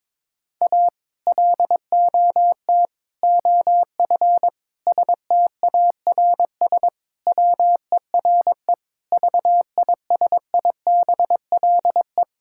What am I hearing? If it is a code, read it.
Morse code, heard as a keyed tone, about 22 wpm, 700 Hz. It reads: A LOT OF STARS WERE VISIBLE
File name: practice / 22wpm / Sentences with 2-Word Phrases - 22wpm